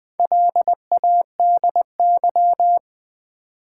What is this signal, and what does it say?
Morse code, heard as a keyed tone, about 20 wpm, 700 Hz. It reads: LADY
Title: Build A Word - 2 Letters At A Time - 4-Letter Words - 20wpm